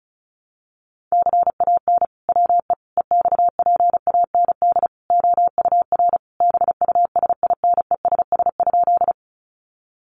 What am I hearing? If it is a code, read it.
Morse code, heard as a keyed tone, about 34 wpm, 700 Hz. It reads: CAN WE EXPAND OUR BUSINESS?
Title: Sentences from Top 1400 Words 35wpm